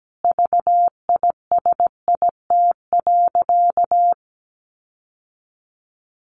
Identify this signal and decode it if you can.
Morse code, heard as a keyed tone, about 17 wpm, 700 Hz. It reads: VISIT.
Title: Sentences from Top 1400 Words 17wpm